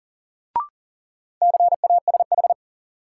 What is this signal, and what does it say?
Morse code, heard as a keyed tone, about 40 wpm, 700 Hz. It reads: CASH
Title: Sentences from Top 700 Words - New Word Drill - 40wpm